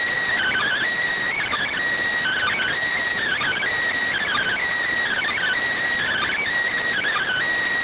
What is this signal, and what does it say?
Emergency Alert System (SAME) tones (header bursts), heard as harsh buzzing bursts. It is heard from 1.1 to 7.0 s.